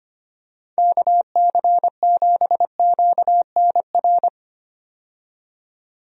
Morse code, heard as a keyed tone, about 25 wpm, 700 Hz. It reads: KC7QNR
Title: USA Call Signs - 6 Characters - 25wpm